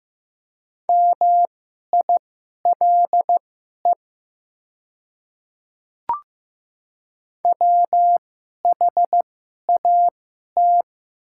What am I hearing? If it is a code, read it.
Morse code, heard as a keyed tone, about 15 wpm, spaced as for 11 wpm, 690 Hz. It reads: MILE WHAT
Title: 2x Character Spacing - Top 300 Words - 15wpm